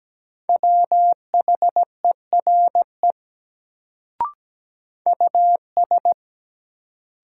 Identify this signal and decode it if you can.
Morse code, heard as a keyed tone, about 17 wpm, 700 Hz. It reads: WHERE US
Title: Sentences from Top 200 Words - New Word Drill - 17wpm